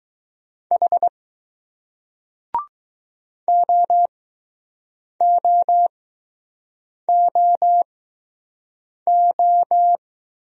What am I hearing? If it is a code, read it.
Morse code, heard as a keyed tone, about 21 wpm, spaced as for 6 wpm, 700 Hz. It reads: H OOOO